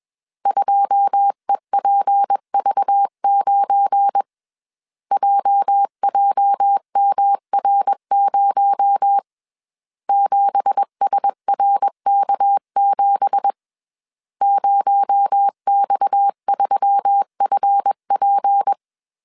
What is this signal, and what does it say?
Morse code, heard as a keyed tone, about 21 wpm, 800 Hz. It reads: 2EPV9 JJMR0 7SRK7 0X3FP
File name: CW / DET / 1 9 1500C S C 20wpm 2310